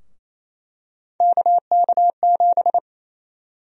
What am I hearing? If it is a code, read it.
Morse code, heard as a keyed tone, about 28 wpm, 700 Hz. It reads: KK7
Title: USA Call Sign Prefixes - Repeated 2x - 28wpm